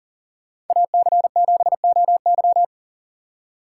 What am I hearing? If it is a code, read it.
Morse code, heard as a keyed tone, about 48 wpm, 700 Hz. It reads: AC7OY
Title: Speed Racing - USA 2x2 Call Signs 40wpm